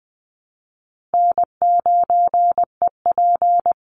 Morse code, heard as a keyed tone, about 20 wpm, 700 Hz. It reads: N9EP
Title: Easy Call Signs Repeated 2x - 20wpm